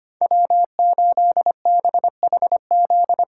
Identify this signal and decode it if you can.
Morse code, heard as a keyed tone, about 25 wpm, 700 Hz. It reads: W8BHZ
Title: USA Callsign DE Callsign - 5 Characters - 25wpm